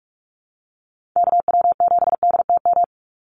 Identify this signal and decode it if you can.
Morse code, heard as a keyed tone, about 44 wpm, 700 Hz. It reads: KW7DTM
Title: USA Call Signs 45wpm